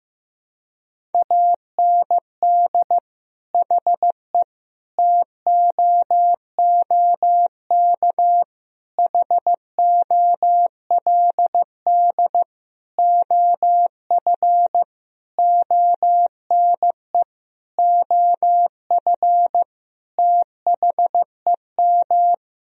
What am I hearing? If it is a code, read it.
Morse code, heard as a keyed tone, about 15 wpm, 700 Hz. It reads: AND HE TOOK HOLD OF ONE OF THEM
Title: Sentences from Top 1000 Words - Review Part 4 - 15wpm